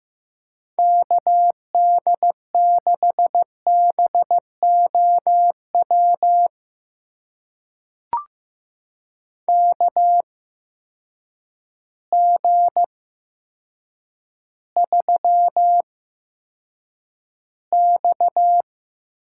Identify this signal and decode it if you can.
Morse code, heard as a keyed tone, about 15 wpm, spaced as for 8 wpm, 700 Hz. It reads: KD6BOW KG3X
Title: USA Call Signs - 8x Character Spacing - 15wpm